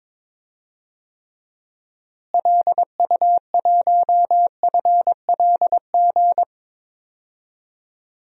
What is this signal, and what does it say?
Morse code, heard as a keyed tone, about 21 wpm, 700 Hz. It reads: LU1FLG
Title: DXCC-100 Call Signs - 6 Characters - Repeated 3x - 22wpm